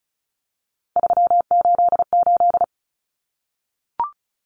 Morse code, heard as a keyed tone, about 34 wpm, 700 Hz. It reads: 388
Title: Common Cut Numbers - 3-Digit 35wpm